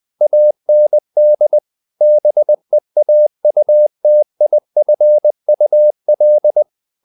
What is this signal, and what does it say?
Morse code, heard as a keyed tone, about 20 wpm, 600 Hz. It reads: AND BEAUTIFUL
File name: the old man and the sea part 2 20wpm